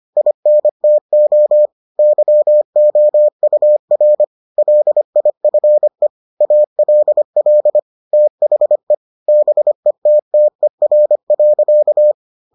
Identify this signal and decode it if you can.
Morse code, heard as a keyed tone, about 25 wpm, 600 Hz. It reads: INTO YOUR LIFE ALL THE BETTER.